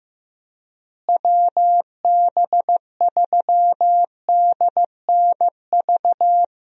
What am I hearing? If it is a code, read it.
Morse code, heard as a keyed tone, about 16 wpm, 700 Hz. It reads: WB3DNV
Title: USA Call Signs - 6 Characters - Repeated 2x - 15wpm